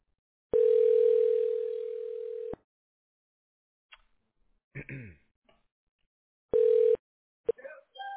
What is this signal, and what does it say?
Ringback tone (North America), 440 + 480 Hz.